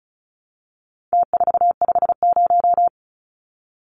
Morse code, heard as a keyed tone, about 34 wpm, 700 Hz. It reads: T450
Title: Numbers - Contest Style - 4-Digit 35wpm